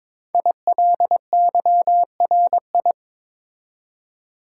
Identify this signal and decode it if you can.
Morse code, heard as a keyed tone, about 22 wpm, 700 Hz. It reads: ILYRI
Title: Code Groups - 5 Letters - 22wpm